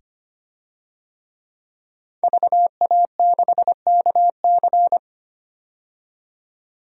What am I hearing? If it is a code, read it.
Morse code, heard as a keyed tone, about 25 wpm, 700 Hz. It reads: VA6KC
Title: DXCC-100 Call Signs - 5 Characters - 25wpm